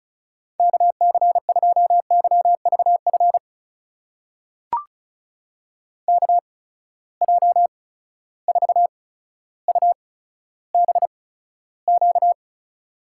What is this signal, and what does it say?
Morse code, heard as a keyed tone, about 36 wpm, spaced as for 15 wpm, 700 Hz. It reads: KC2YVF KJ4UBQ
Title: USA Call Signs - 6 Characters - 8x Character Spacing - 35wpm